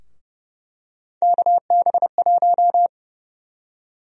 Morse code, heard as a keyed tone, about 30 wpm, 700 Hz. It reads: KB1
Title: USA Call Sign Prefixes - 30wpm